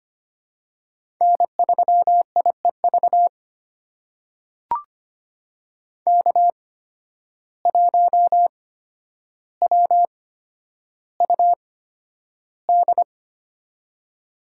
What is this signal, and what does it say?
Morse code, heard as a keyed tone, about 25 wpm, spaced as for 11 wpm, 700 Hz. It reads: N3IEV K1WUD
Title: USA Call Signs - 8x Character Spacing - 25wpm